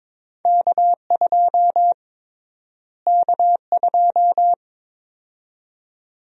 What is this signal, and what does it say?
Morse code, heard as a keyed tone, about 22 wpm, 700 Hz. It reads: K2 K2
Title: USA Call Sign Prefixes - Repeated 2x - 22wpm